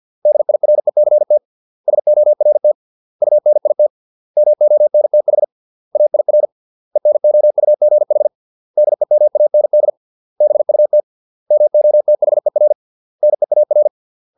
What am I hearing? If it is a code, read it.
Morse code, heard as a keyed tone, about 50 wpm, 600 Hz. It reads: DIRECT SORT UNIT MONTH AIR ENOUGH DEMAND BUT MOTHER NEAR